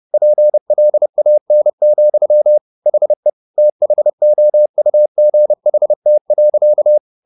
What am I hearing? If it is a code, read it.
Morse code, heard as a keyed tone, about 30 wpm, 600 Hz. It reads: PLAN, HE THOUGHT.
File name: the old man and the sea / the old man and the sea 1 30wpm